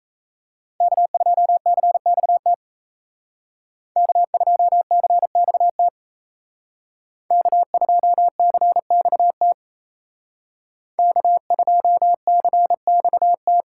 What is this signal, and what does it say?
Morse code, heard as a keyed tone, about 32 wpm, 700 Hz. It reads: K2CXT K2CXT K2CXT K2CXT
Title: Speed Racing - USA 1x3 Call Signs 28wpm